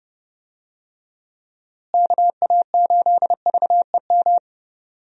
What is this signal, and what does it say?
Morse code, heard as a keyed tone, about 32 wpm, 700 Hz. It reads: KA8VEM
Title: USA Call Signs - 6 Characters - Repeated 2x - 30wpm